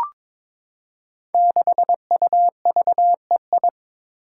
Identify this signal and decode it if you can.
Morse code, heard as a keyed tone, about 22 wpm, 700 Hz. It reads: 6UVEI
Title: Five Challenging Letter-Number Combinations 22wpm